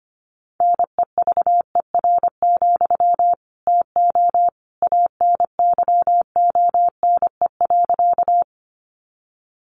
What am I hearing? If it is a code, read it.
Morse code, heard as a keyed tone, about 25 wpm, 700 Hz. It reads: NEVER, TO ANYONE.
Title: Sentences from Top 900 Words 25wpm